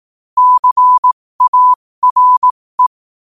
Morse code, heard as a keyed tone, about 18 wpm, spaced as for 15 wpm, 1000 Hz. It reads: CARE
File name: qotdfiles / 15wpm